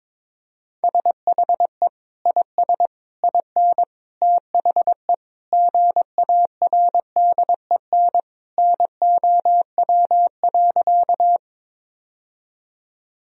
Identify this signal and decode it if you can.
Morse code, heard as a keyed tone, about 22 wpm, 700 Hz. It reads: SHE IS IN THE GARDEN NOW.
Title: Sentences from Top 1000 Words - Review Part 4 - 22wpm